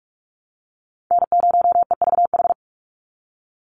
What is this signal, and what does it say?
Morse code, heard as a keyed tone, about 44 wpm, 700 Hz. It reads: N0EVH
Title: NA SOTA Chaser Callsigns 45wpm